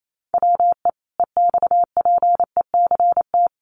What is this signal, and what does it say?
Morse code, heard as a keyed tone, about 28 wpm, 700 Hz. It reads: WE EXPECT
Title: Sentences from Top 1100 Words 28wpm